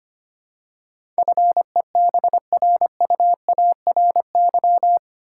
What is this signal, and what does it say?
Morse code, heard as a keyed tone, about 25 wpm, 700 Hz. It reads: FEBRUARY
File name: 8 Letter Words - First 3 Letters Given - 25wpm